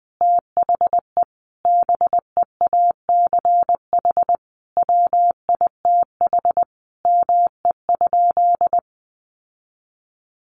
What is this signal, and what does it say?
Morse code, heard as a keyed tone, about 20 wpm, 700 Hz. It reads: THE BEACH WITH ME?
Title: Sentences from Top 1500 Words 20wpm